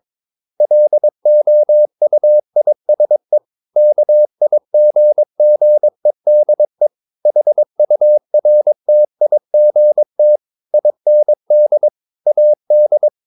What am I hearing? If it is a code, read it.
Morse code, heard as a keyed tone, about 22 wpm, 600 Hz. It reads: LOUISE KIGGEDE HURTIGT IND AD
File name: Ferie QRS 22wpm